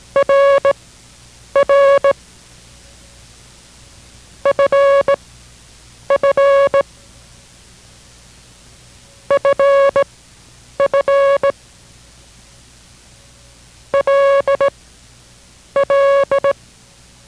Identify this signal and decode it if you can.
Morse code, heard as a keyed tone, about 18 wpm, spaced as for 8 wpm, 560 Hz. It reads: RR FF FF LL